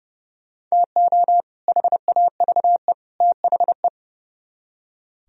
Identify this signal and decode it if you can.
Morse code, heard as a keyed tone, about 37 wpm, 700 Hz. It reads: TO HAVE THE
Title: Speed-Racing - Sets of 3 Words 30wpm